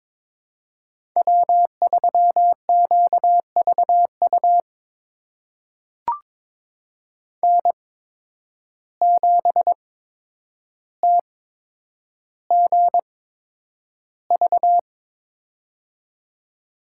Morse code, heard as a keyed tone, about 22 wpm, spaced as for 9 wpm, 700 Hz. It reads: W3QVU N7TGV
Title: USA Call Signs - 5 Characters - 8x Character Spacing - 22wpm